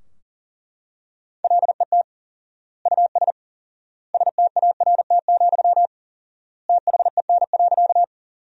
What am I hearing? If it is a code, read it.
Morse code, heard as a keyed tone, about 40 wpm, 700 Hz. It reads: LET US START, THEN.
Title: 4x Word-Spacing - Sentences from Top 300 Words - 40wpm